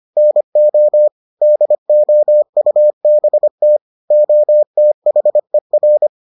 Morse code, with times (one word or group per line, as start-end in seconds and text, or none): 0.2-1.1 NO
1.4-3.8 DOUBT
4.1-6.1 OTHER